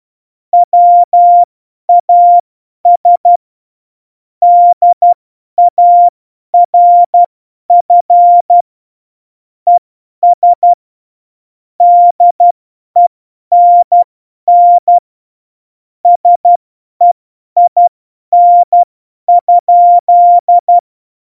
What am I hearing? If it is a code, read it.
Morse code, heard as a keyed tone, about 12 wpm, spaced as for 10 wpm, 700 Hz. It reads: WAS DARF ES DENN SEIN?
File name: Morsekurs / Witze / Witz 20. November 2018, 50 F70